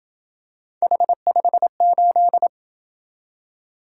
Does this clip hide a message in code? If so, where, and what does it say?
Yes: Morse code from 0.8 to 2.5 s, heard as a keyed tone, about 26 wpm, 700 Hz. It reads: H58